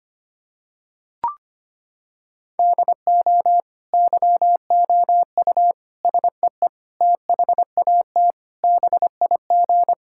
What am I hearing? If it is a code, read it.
Morse code, heard as a keyed tone, about 25 wpm, 700 Hz. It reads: DO YOU SEE THAT BIG
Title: Sentences from Top 1000 Words - Review Part 5 - 25wpm